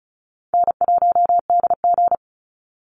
Morse code, heard as a keyed tone, about 34 wpm, 700 Hz. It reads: N1DG
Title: Easy Call Signs Repeated 3x - 35wpm